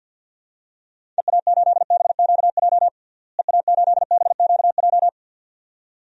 Morse code, heard as a keyed tone, about 46 wpm, 700 Hz. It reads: EA8BQJ EA8BQJ
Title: DXCC-100 Call Signs - 6 Characters - Repeated 2x - 50wpm